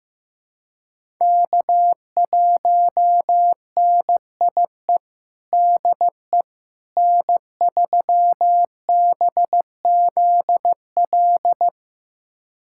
Morse code, heard as a keyed tone, about 15 wpm, 700 Hz. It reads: K1NIE DE N3BZL
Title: USA Callsign DE Callsign - 5 Characters - 15wpm